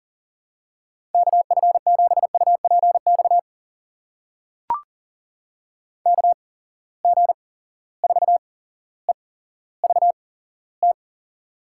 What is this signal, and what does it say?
Morse code, heard as a keyed tone, about 40 wpm, spaced as for 17 wpm, 700 Hz. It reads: KF7UPX KG4EVT
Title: USA Call Signs - 6 Characters - 8x Character Spacing - 40wpm